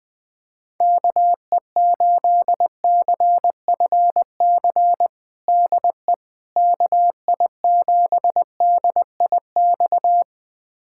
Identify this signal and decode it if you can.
Morse code, heard as a keyed tone, about 21 wpm, 700 Hz. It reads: KE8CFC DE KI7DIX
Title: USA Callsign DE Callsign - 6 Characters - 20wpm